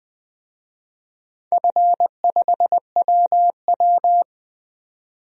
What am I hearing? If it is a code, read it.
Morse code, heard as a keyed tone, about 20 wpm, 700 Hz. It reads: F5WW
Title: DXCC-100 Call Signs - 4 Characters - 20wpm